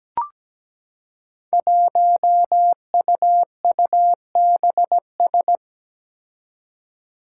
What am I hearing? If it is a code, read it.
Morse code, heard as a keyed tone, about 17 wpm, 700 Hz. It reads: 1UUBS
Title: Five Challenging Letter-Number Combinations 17wpm